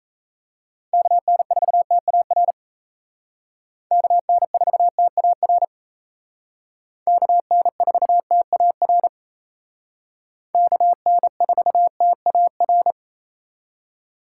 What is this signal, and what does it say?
Morse code, heard as a keyed tone, about 37 wpm, 700 Hz. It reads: KN4TAR KN4TAR KN4TAR KN4TAR